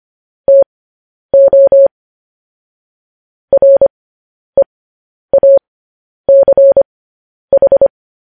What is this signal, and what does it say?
Morse code, heard as a keyed tone, about 25 wpm, spaced as for 10 wpm, 550 Hz. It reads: TO REACH
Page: news headlines in morse code at 10 wpm